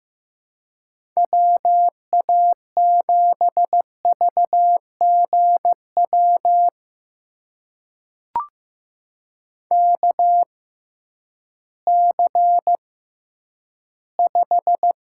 Morse code, heard as a keyed tone, about 15 wpm, spaced as for 10 wpm, 700 Hz. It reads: WA7VGW KC5
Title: USA Call Signs - 6 Characters - 6x Character Spacing - 15wpm